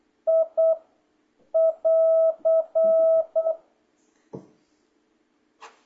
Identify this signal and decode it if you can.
Morse code, heard as a keyed tone, about 8 wpm, 650 Hz. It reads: I <AR>